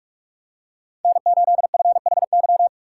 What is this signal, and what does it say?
Morse code, heard as a keyed tone, about 42 wpm, 700 Hz. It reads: N8FHY
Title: USA Call Signs - 5 Characters - 4x Character Spacing - 45wpm